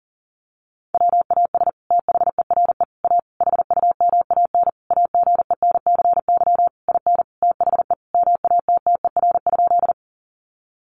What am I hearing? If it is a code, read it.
Morse code, heard as a keyed tone, about 40 wpm, 700 Hz. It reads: WAS THERE A HUMAN AGENCY IN THE MATTER?
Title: Sentences from Top 1300 Words 40wpm